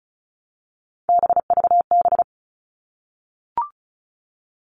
Morse code, heard as a keyed tone, about 34 wpm, 700 Hz. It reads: BVB
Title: All Cut Numbers - 3-Digit 35wpm